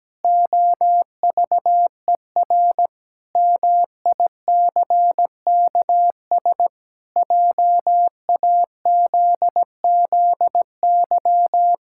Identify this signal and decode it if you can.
Morse code, heard as a keyed tone, about 17 wpm, 700 Hz. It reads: OVER MICKS JAZZY
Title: Pangram Challenge v2 - 17wpm